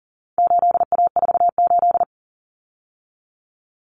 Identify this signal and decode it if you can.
Morse code, heard as a keyed tone, about 40 wpm, 700 Hz. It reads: 8A48
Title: Common Cut Numbers - 4-Digit 40wpm